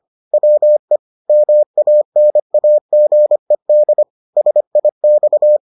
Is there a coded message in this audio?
Yes: Morse code from 0.3 to 5.6 s, heard as a keyed tone, about 25 wpm, 600 Hz. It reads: WE MANAGED SIX